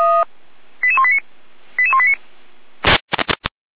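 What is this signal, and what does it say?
Five-tone selective call (ZVEI), heard as quick runs of five tones, tones of about 70 ms. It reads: 88189 88189
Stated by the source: ZVEI five-tone selective call received on an SDR.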